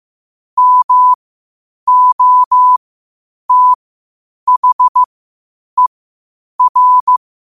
Morse code, heard as a keyed tone, about 15 wpm, spaced as for 8 wpm, 1000 Hz. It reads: MOTHER
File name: qotdfiles / 8wpm